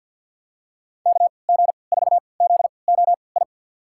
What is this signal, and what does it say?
Morse code, heard as a keyed tone, about 52 wpm, spaced as for 29 wpm, 690 Hz. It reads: KG4ZOI
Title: USA Call Signs - 6 Characters - 3x Character Spacing - 50wpm